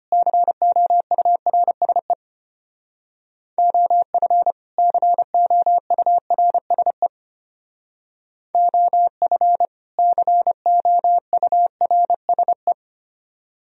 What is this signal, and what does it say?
Morse code, heard as a keyed tone, about 32 wpm, 700 Hz. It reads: COURSE OF COURSE OF COURSE